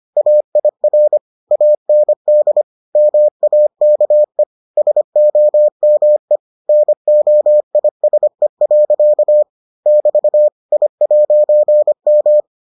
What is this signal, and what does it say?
Morse code, heard as a keyed tone, about 25 wpm, 600 Hz. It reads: AIR AND MAKE SOME NOISE. <BT> I'M